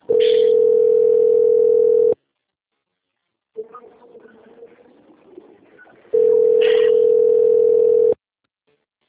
Ringback tone (North America), 440 + 480 Hz.